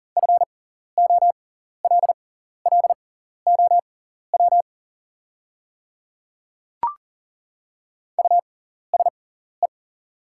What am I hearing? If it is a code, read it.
Morse code, heard as a keyed tone, about 40 wpm, spaced as for 14 wpm, 700 Hz. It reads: FOLLOW USE
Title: 6x Character Spacing - Top 200 Words - 40wpm